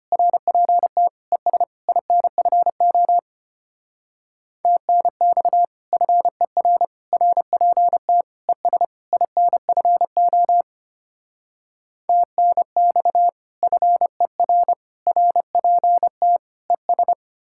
Morse code, heard as a keyed tone, about 33 wpm, 700 Hz. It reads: RPT ES INFO TNX FER RPT ES INFO TNX FER RPT ES